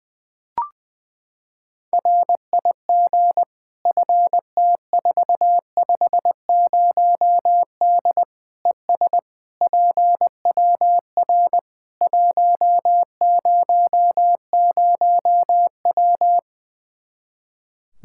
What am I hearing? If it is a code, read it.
Morse code, heard as a keyed tone, about 21 wpm, 700 Hz. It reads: RIG FT450D ES PWR 100W
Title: Practice QSO - Exchange 2 - Info - 20wpm